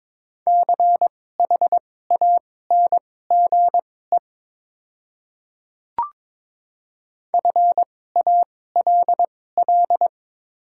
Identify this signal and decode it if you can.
Morse code, heard as a keyed tone, about 22 wpm, spaced as for 16 wpm, 690 Hz. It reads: CHANGE FALL
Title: 2x Character Spacing - Top 500 Words - 22wpm